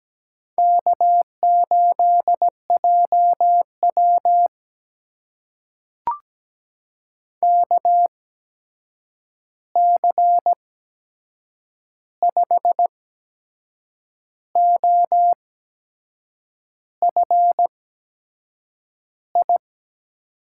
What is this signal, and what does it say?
Morse code, heard as a keyed tone, about 17 wpm, spaced as for 6 wpm, 700 Hz. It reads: K8JW KC5OFI
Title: USA Call Signs - 8x Character Spacing - 17wpm